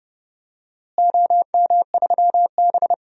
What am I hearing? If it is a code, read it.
Morse code, heard as a keyed tone, about 30 wpm, 700 Hz. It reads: OM3B
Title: DXCC-100 Call Signs - 4 Characters - Repeated 3x - 30wpm